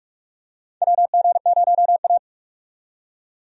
Morse code, heard as a keyed tone, about 44 wpm, 700 Hz. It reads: WG0A
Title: USA Call Signs - 4 Characters - 45wpm